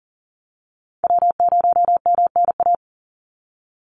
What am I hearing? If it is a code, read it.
Morse code, heard as a keyed tone, about 40 wpm, 700 Hz. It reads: W0MNA
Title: NA SOTA Chaser Callsigns 40wpm